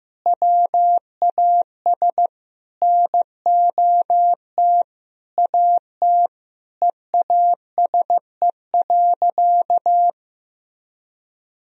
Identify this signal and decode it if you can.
Morse code, heard as a keyed tone, about 15 wpm, 700 Hz. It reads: WAS NOT AT EASE.
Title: Sentences from Top 1000 Words - Review Part 1 - 15wpm